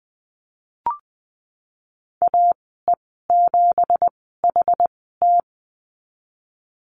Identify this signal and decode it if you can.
Morse code, heard as a keyed tone, about 20 wpm, spaced as for 14 wpm, 700 Hz. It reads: AE7HT
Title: 245 Lesson 39g - Letter z - Call Signs 20wpm